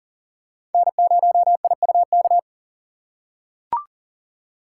Morse code, heard as a keyed tone, about 40 wpm, 700 Hz. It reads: N0IUK
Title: USA Call Signs - 5 Characters - 6x Character Spacing - 40wpm